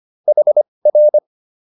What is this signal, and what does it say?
Morse code, heard as a keyed tone, about 25 wpm, spaced as for 20 wpm, 600 Hz. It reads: HR